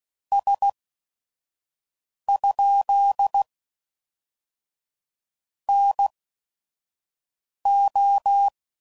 Morse code, heard as a keyed tone, about 16 wpm, spaced as for 5 wpm, 800 Hz. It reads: S? NO
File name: CW Code Groups @5WPM / Technical Subjects / AND Gate